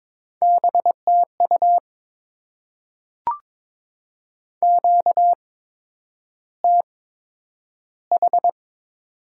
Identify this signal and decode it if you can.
Morse code, heard as a keyed tone, about 22 wpm, spaced as for 9 wpm, 700 Hz. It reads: BTU QTH
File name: Top 20 QSO Elements - 8x Character Spacing - 22wpm